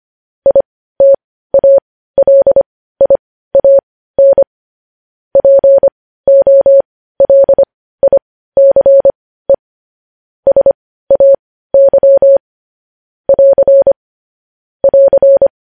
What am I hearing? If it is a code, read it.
Morse code, heard as a keyed tone, about 25 wpm, spaced as for 15 wpm, 550 Hz. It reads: ITALIAN POLICE SAY <AR> <AR>
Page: news headlines in morse code at 15 wpm